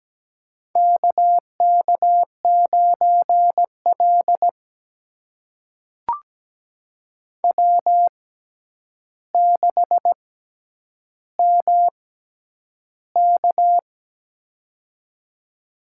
Morse code, heard as a keyed tone, about 17 wpm, spaced as for 9 wpm, 700 Hz. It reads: KK9L W6MK
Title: USA Call Signs - 4 Characters - 6x Character Spacing - 17wpm